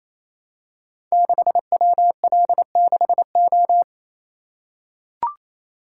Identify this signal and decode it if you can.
Morse code, heard as a keyed tone, about 28 wpm, 700 Hz. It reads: 6WL6O
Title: Code Groups - 5 Letter:Numbers - 28wpm